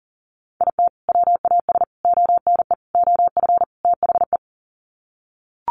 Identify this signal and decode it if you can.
Morse code, heard as a keyed tone, about 40 wpm, 700 Hz. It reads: IT WAS ONE OF THE
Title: Sets of 5 Words 40wpm